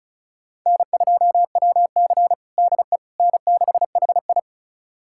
Morse code, heard as a keyed tone, about 36 wpm, 700 Hz. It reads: N2WC DE N6HI
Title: USA Callsign DE Callsign - 4 Characters - 35wpm